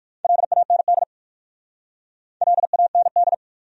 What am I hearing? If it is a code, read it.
Morse code, heard as a keyed tone, about 55 wpm, 690 Hz. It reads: LAND LAND
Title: Speed-Racing Top 500 Words 45wpm